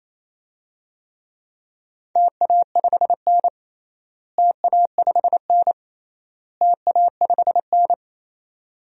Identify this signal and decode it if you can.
Morse code, heard as a keyed tone, about 27 wpm, 700 Hz. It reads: TA5N TA5N TA5N
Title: DXCC-100 Call Signs - 4 Characters - Repeated 3x - 28wpm